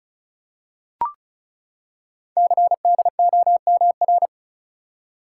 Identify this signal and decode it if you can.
Morse code, heard as a keyed tone, about 36 wpm, 690 Hz. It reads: CDOMR
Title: Code Groups - 5 Letters - 35wpm